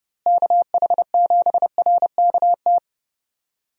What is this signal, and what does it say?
Morse code, heard as a keyed tone, about 30 wpm, 700 Hz. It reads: KH7RKT
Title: USA Call Signs - 6 Characters - Repeated 3x - 30wpm